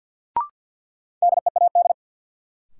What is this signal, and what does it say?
Morse code, heard as a keyed tone, about 53 wpm, 700 Hz. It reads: DEAD
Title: Sentences from Top 1500 Words - New Word Drill - 50wpm